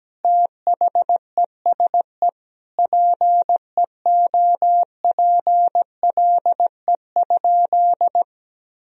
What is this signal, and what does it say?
Morse code, heard as a keyed tone, about 17 wpm, 700 Hz. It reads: THESE PEOPLE?
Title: Sentences from Top 200 Words and Abbreviations 17wpm